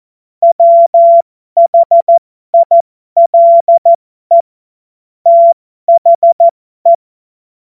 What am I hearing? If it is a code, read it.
Morse code, heard as a keyed tone, about 14 wpm, spaced as for 12 wpm, 680 Hz. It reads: WHILE THE
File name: Morsekurs / Witze